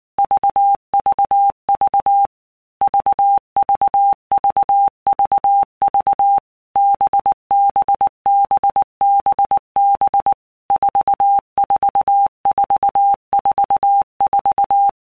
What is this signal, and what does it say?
Morse code, heard as a keyed tone, about 19 wpm, 800 Hz. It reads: VVV VVVVV BBBBB 44444
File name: Basic Letter Groups By Blocks / Groups@18 / VB46@18